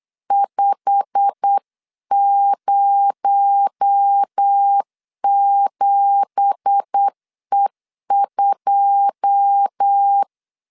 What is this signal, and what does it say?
Morse code, heard as a keyed tone, about 8 wpm, 800 Hz. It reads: 507E2
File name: CW / DET / 1 9 1500C S C 8wpm 5653